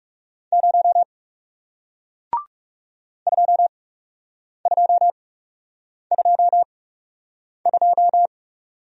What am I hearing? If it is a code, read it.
Morse code, heard as a keyed tone, about 40 wpm, spaced as for 8 wpm, 700 Hz. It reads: 0 2222